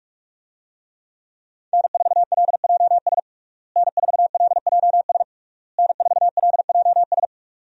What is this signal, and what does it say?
Morse code, heard as a keyed tone, about 41 wpm, 700 Hz. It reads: N4LJS N4LJS N4LJS
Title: USA Call Signs - 5 Characters - Repeated 3x - 45wpm